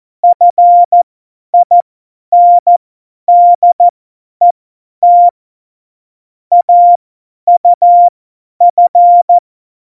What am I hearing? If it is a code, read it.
Morse code, heard as a keyed tone, about 14 wpm, spaced as for 10 wpm, 700 Hz. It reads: FINDET AUF